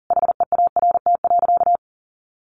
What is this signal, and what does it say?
Morse code, heard as a keyed tone, about 37 wpm, 700 Hz. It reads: HEART.
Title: Sentences from Top 1400 Words 40wpm